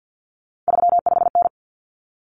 Morse code, heard as a keyed tone, about 50 wpm, 700 Hz. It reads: 35N
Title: Common Cut Numbers - 3-Digit 50wpm